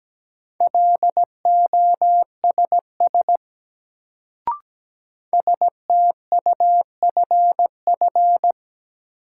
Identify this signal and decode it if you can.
Morse code, heard as a keyed tone, about 17 wpm, 700 Hz. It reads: LOSS STUFF
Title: Sentences from Top 1400 Words - New Word Drill Alt - 17wpm